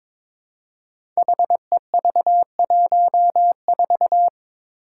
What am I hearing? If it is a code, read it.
Morse code, heard as a keyed tone, about 22 wpm, 700 Hz. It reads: HEV14
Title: Five Challenging Letter-Number Combinations 22wpm